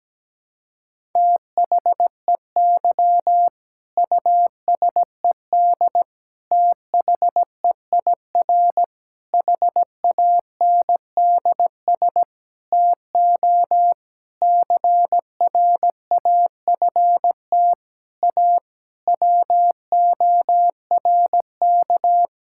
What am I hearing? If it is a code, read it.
Morse code, heard as a keyed tone, about 17 wpm, 700 Hz. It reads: THEY USED THEIR HANDS TO CRAFT A WORK
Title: Sentences from Top 1000 Words - Review Part 6 - 17wpm